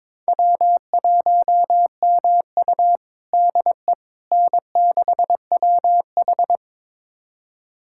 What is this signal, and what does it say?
Morse code, heard as a keyed tone, about 22 wpm, 700 Hz. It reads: W1MU DE N6WH
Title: USA Callsign DE Callsign - 4 Characters - 22wpm